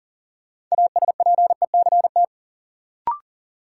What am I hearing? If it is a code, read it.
Morse code, heard as a keyed tone, about 38 wpm, 700 Hz. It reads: ASPECT
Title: Sentences from Top 1300 Words - New Word Drill - 40wpm